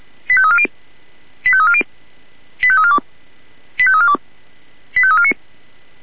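Five-tone selective call (ZVEI), heard as quick runs of five tones, tones of about 70 ms. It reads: 85340 85340 85342 85342 85349
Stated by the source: ZVEI five-tone selective call received by SDR.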